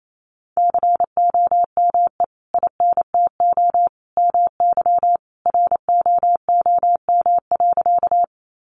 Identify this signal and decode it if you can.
Morse code, heard as a keyed tone, about 28 wpm, 700 Hz. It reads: COME INTO MY ROOM.
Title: Sentences from Top 600 Words 28wpm